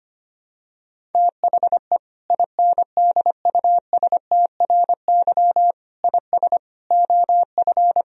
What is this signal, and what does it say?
Morse code, heard as a keyed tone, about 25 wpm, 700 Hz. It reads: THE INDUSTRY IS OF
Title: Sentences from Top 1000 Words - Review Part 4 - 25wpm